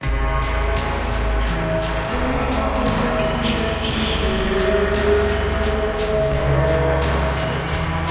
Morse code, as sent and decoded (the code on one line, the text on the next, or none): none